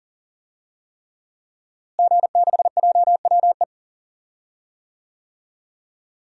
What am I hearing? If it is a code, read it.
Morse code, heard as a keyed tone, about 40 wpm, 700 Hz. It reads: G6JWE
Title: DXCC-100 Call Signs - 5 Characters - Repeated 3x - 40wpm